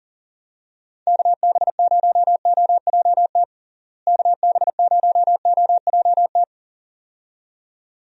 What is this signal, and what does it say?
Morse code, heard as a keyed tone, about 40 wpm, 700 Hz. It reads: KB0OJT KB0OJT
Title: USA Call Signs - 6 Characters - Repeated 2x - 40wpm